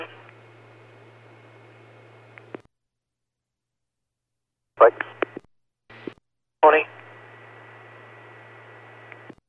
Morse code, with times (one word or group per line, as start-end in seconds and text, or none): none